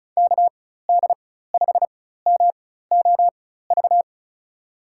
Morse code, heard as a keyed tone, about 34 wpm, spaced as for 16 wpm, 690 Hz. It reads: KD5MOV
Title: USA Call Signs - 4x Character Spacing - 35wpm